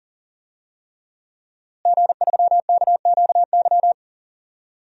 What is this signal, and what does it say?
Morse code, heard as a keyed tone, about 40 wpm, 700 Hz. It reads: G3KQY
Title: DXCC-100 Call Signs - 5 Characters - Repeated 3x - 40wpm